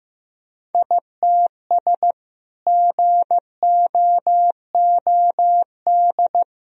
Morse code, heard as a keyed tone, about 15 wpm, 700 Hz. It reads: ITS GOOD
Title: Sentences from Top 1000 Words - Review Part 4 - 15wpm